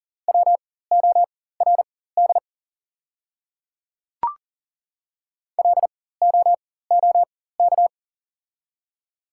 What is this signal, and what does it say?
Morse code, heard as a keyed tone, about 38 wpm, spaced as for 18 wpm, 700 Hz. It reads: WORD LOOK
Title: 4x Character Spacing - Top 200 Words - 40wpm